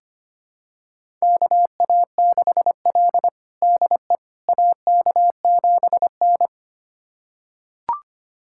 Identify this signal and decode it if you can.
Morse code, heard as a keyed tone, about 25 wpm, 700 Hz. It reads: KA6L DE AK7N
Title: USA Callsign DE Callsign - 4 Characters - 25wpm